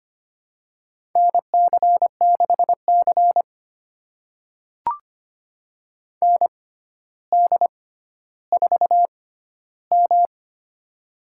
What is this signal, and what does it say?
Morse code, heard as a keyed tone, about 25 wpm, spaced as for 13 wpm, 700 Hz. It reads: NC6C ND4M
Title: USA Call Signs - 6x Character Spacing - 25wpm